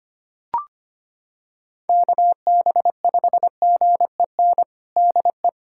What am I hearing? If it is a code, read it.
Morse code, heard as a keyed tone, about 25 wpm, 700 Hz. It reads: KB5GEN DE
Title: USA Callsign DE Callsign - 6 Characters - 25wpm